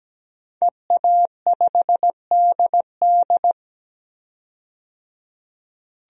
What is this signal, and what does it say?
Morse code, heard as a keyed tone, about 17 wpm, 700 Hz. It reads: EA5DD
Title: DXCC-100 Call Signs - 5 Characters - 17wpm